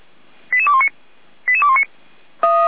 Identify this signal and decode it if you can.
Five-tone selective call (ZVEI), heard as quick runs of five tones, tones of about 70 ms. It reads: 88219 88219